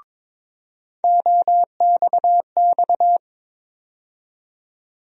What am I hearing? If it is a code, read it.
Morse code, heard as a keyed tone, about 22 wpm, 700 Hz. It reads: OXX
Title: USA Call Sign Postfixes - 3 Letters 22wpm